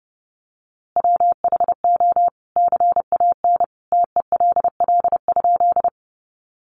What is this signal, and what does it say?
Morse code, heard as a keyed tone, about 30 wpm, 700 Hz. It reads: WHO CAN TELL?
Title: A Sound Of Thunder 30wpm